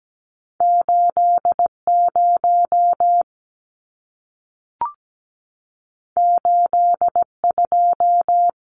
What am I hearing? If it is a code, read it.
Morse code, heard as a keyed tone, about 17 wpm, 700 Hz. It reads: 80 82
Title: Numbers - 2-Digit 17wpm